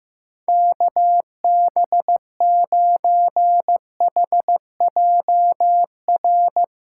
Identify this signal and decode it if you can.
Morse code, heard as a keyed tone, about 15 wpm, 700 Hz. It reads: KB9HJR
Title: USA Callsign DE Callsign - 6 Characters - 15wpm